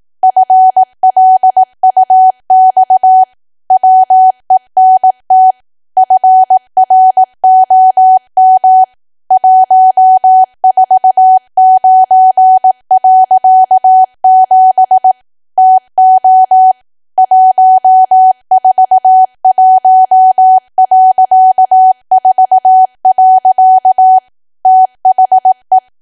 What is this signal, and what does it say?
Morse code, heard as a keyed tone, about 18 wpm, 750 Hz. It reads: FLUX WENT FROM 149.7 TO 141.4. THE
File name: cwa / W1AW-Level2 / 18WPM